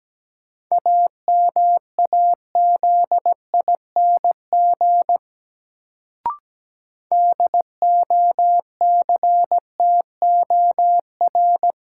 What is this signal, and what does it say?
Morse code, heard as a keyed tone, about 17 wpm, 700 Hz. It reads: AMAZING DOCTOR
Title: Sentences from Top 1500 Words - New Word Drill Alt - 17wpm